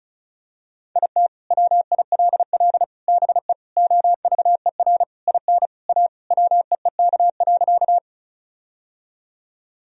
Morse code, heard as a keyed tone, about 36 wpm, 700 Hz. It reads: IT WILL BE OVER IN A WEEK.
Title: Sentences from Top 1000 Words - Review Part 2 - 35wpm